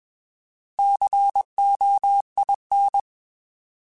Morse code, heard as a keyed tone, about 21 wpm, 800 Hz. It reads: COIN
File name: Alphabetical / Alphabetical mp3s / Cs @20